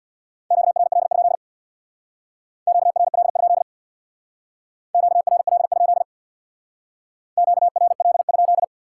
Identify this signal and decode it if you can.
Morse code, heard as a keyed tone, about 65 wpm, 700 Hz. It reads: QRL? QRL? QRL? QRL?